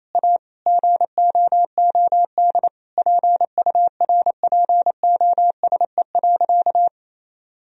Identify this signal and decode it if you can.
Morse code, heard as a keyed tone, about 28 wpm, 700 Hz. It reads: A GOOD PURPOSE.
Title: Sentences from Top 1000 Words - Review Part 2 - 28wpm